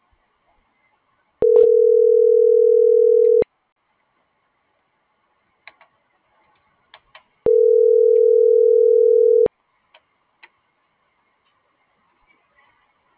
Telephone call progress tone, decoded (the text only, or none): ringback tone